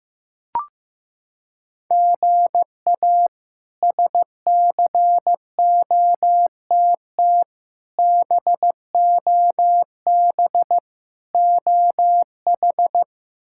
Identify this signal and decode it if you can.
Morse code, heard as a keyed tone, about 16 wpm, 700 Hz. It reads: GA SCOTT BOB OH
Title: K1USN SST QSO Practice - 15wpm